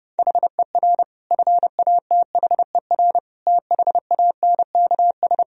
Morse code, heard as a keyed tone, about 32 wpm, 700 Hz. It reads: HER FATHER THANKS
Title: Sentences from Top 1000 Words - Review Part 3 - 30wpm